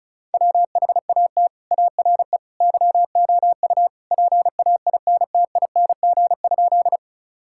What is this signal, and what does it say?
Morse code, heard as a keyed tone, about 36 wpm, 700 Hz. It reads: WHAT ARE YOU PAINTING?
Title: Sentences from Top 1000 Words - Review Part 6 - 35wpm